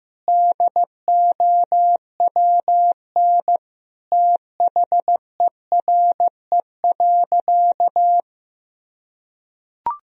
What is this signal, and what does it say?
Morse code, heard as a keyed tone, about 16 wpm, 700 Hz. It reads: DOWN THERE.
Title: Sentences from Top 1000 Words - Review Part 4 - 15wpm